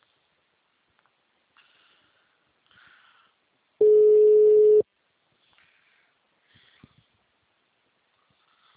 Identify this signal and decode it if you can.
Ringback tone (Europe, 425 Hz).